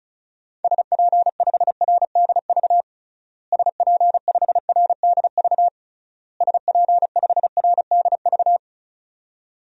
Morse code, heard as a keyed tone, about 34 wpm, 700 Hz. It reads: SP5RDV SP5RDV SP5RDV
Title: DXCC-100 Call Signs - 6 Characters - Repeated 3x - 35wpm